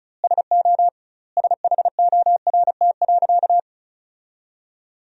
Morse code, heard as a keyed tone, about 34 wpm, 700 Hz. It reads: SO SHORT.
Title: 2x Word-Spacing - Sentences from Top 500 Words - 35wpm